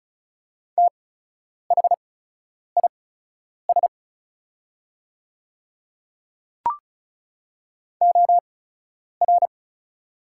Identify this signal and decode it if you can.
Morse code, heard as a keyed tone, about 34 wpm, spaced as for 10 wpm, 700 Hz. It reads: THIS OR
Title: 8x Character Spacing - Top 100 Words - 35wpm